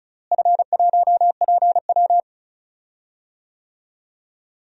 Morse code, heard as a keyed tone, about 34 wpm, 700 Hz. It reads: F1PW
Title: DXCC-100 Call Signs - 4 Characters - Repeated 2x - 35wpm